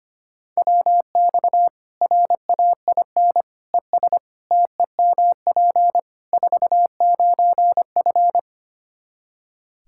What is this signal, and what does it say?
Morse code, heard as a keyed tone, about 25 wpm, 700 Hz. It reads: WX RAIN ES TEMP 49F
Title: Practice QSO - Exchange 2 - Info - 25wpm